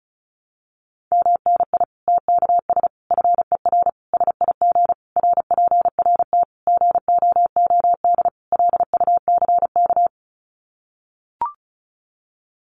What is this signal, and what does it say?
Morse code, heard as a keyed tone, about 34 wpm, 700 Hz. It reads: MNI TKS FER SIG RPRT GOOD LUCK
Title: Common QSO Phrases 35wpm